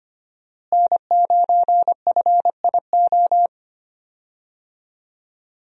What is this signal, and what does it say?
Morse code, heard as a keyed tone, about 25 wpm, 700 Hz. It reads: N9FIO
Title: USA Call Signs - 5 Characters - Repeated 3x - 25wpm